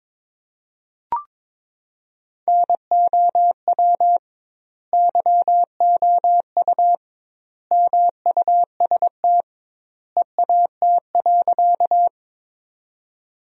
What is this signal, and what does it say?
Morse code, heard as a keyed tone, about 21 wpm, 700 Hz. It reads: NOW YOU MUST EAT.
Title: 2x Word-Spacing - Sentences from Top 300 Words - 22wpm